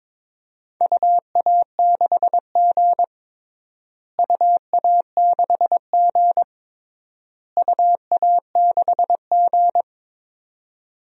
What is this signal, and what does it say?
Morse code, heard as a keyed tone, about 22 wpm, 700 Hz. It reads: UA6G UA6G UA6G
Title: DXCC-100 Call Signs - 4 Characters - Repeated 3x - 22wpm